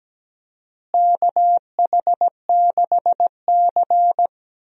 Morse code, heard as a keyed tone, about 17 wpm, 700 Hz. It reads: KH6C
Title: DXCC-100 Call Signs - 4 Characters - Repeated 3x - 17wpm